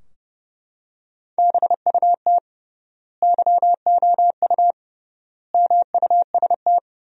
Morse code, heard as a keyed tone, about 30 wpm, 700 Hz. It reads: BUT YOU MUST
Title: 3x Word-Spacing - Sentences from Top 300 Words - 30wpm